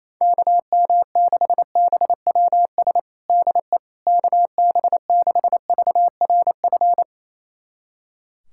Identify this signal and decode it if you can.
Morse code, heard as a keyed tone, about 28 wpm, 700 Hz. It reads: KM6BWS DE KB6VRF
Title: USA Callsign DE Callsign - 6 Characters - 28wpm